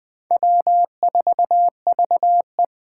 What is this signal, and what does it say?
Morse code, heard as a keyed tone, about 21 wpm, 700 Hz. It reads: W4VE